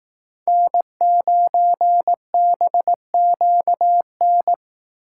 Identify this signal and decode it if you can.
Morse code, heard as a keyed tone, about 19 wpm, 700 Hz. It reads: N9BQN